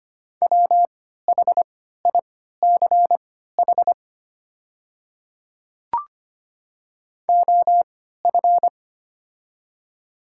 Morse code, heard as a keyed tone, about 25 wpm, spaced as for 14 wpm, 690 Hz. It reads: WHICH OF
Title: 3x Character Spacing - Top 100 Words - 25wpm